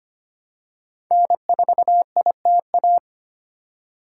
Morse code, heard as a keyed tone, about 25 wpm, 700 Hz. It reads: N4ITA
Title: USA Call Signs - 5 Characters - 3x Character Spacing - 25wpm